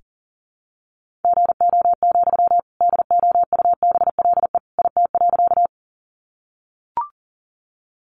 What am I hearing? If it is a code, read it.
Morse code, heard as a keyed tone, about 40 wpm, 700 Hz. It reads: GO, DOUBLE IT.
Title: Sentences from Top 1400 Words 40wpm